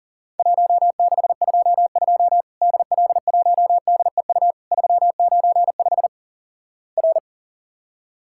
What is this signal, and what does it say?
Morse code, heard as a keyed tone, about 43 wpm, 700 Hz. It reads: 1622 DL1DEU 395 R
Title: Worked All Europe DX Contest - QTC Exchange - 40wpm